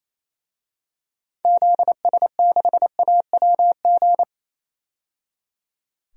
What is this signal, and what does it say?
Morse code, heard as a keyed tone, about 28 wpm, 700 Hz. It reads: ZS6AWG
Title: DXCC-100 Call Signs - 6 Characters - 28wpm